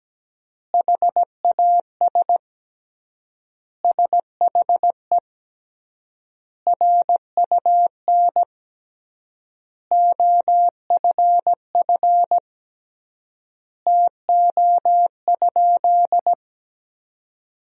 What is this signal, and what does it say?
Morse code, heard as a keyed tone, about 17 wpm, 700 Hz. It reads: HAS SHE RUN OFF TO?
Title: 3x Word-Spacing - Sentences from Top 300 Words - 17wpm